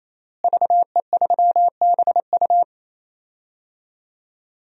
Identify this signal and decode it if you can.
Morse code, heard as a keyed tone, about 28 wpm, 700 Hz. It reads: VE3BU
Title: DXCC-100 Call Signs - 5 Characters - Repeated 3x - 28wpm